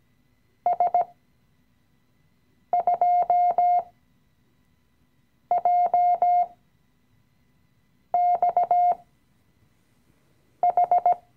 Morse code, heard as a keyed tone, about 17 wpm, spaced as for 5 wpm, 710 Hz. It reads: S2JXH
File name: CW-Online-Kurs / Anfänger / Lektion 017